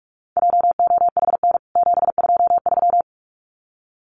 Morse code, heard as a keyed tone, about 44 wpm, 700 Hz. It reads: JOHN 723
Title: CWT Exchange 45wpm